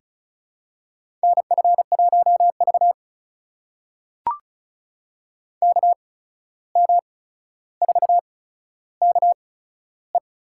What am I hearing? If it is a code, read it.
Morse code, heard as a keyed tone, about 36 wpm, spaced as for 14 wpm, 700 Hz. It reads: NF1V KM4KE
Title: USA Call Signs - 8x Character Spacing - 35wpm